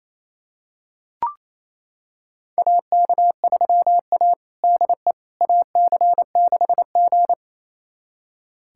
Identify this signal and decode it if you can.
Morse code, heard as a keyed tone, about 28 wpm, 700 Hz. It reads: AK3A DE AC6G
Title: USA Callsign DE Callsign - 4 Characters - 28wpm